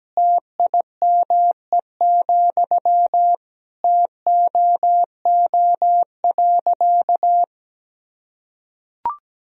Morse code, heard as a keyed tone, about 17 wpm, 700 Hz. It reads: TIME, TOO.
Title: Sentences from Top 1000 Words - Review Part 1 - 17wpm